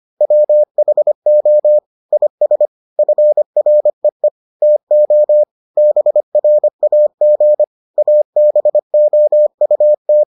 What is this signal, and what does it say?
Morse code, heard as a keyed tone, about 25 wpm, 600 Hz. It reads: WHO IS FREE TO BRAG ABOUT